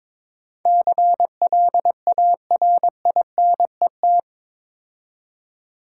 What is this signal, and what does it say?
Morse code, heard as a keyed tone, about 25 wpm, 700 Hz. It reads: CLARINET
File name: Easy Words - Speed-Racing with Hints - 22wpm